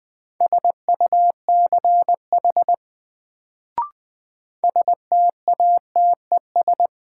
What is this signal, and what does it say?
Morse code, heard as a keyed tone, about 20 wpm, 700 Hz. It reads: SUCH STATES
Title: Sets of 2 Words - New Word Drill - 20wpm